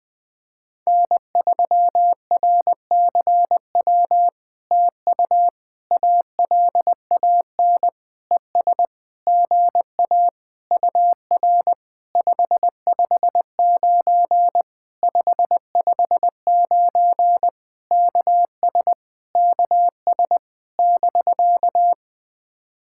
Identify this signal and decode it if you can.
Morse code, heard as a keyed tone, about 20 wpm, 700 Hz. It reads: N3RCW TU ALAN ES GA UR 559 559 KS KS <BK>
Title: POTA QSO Practice - 20wpm